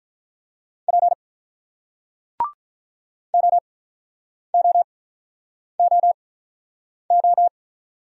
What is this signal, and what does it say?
Morse code, heard as a keyed tone, about 48 wpm, 700 Hz. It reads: P O O O O